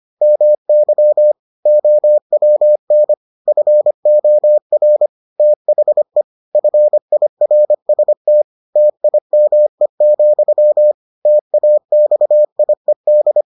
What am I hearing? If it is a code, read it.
Morse code, heard as a keyed tone, about 25 wpm, 600 Hz. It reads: MY OWN FOR THE FIRST TIME, TAXIED